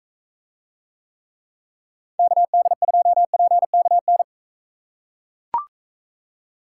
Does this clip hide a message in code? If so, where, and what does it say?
Yes: Morse code from 2.2 to 4.2 s, heard as a keyed tone, about 38 wpm, 700 Hz. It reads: KD2PKN